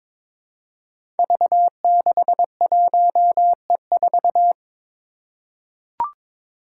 Morse code, heard as a keyed tone, about 23 wpm, 700 Hz. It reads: V61E4
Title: Five Challenging Letter-Number Combinations 22wpm